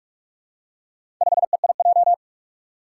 Morse code, heard as a keyed tone, about 46 wpm, 700 Hz. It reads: 5EIJ
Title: Four Challenging Letter-Number Combinations 45wpm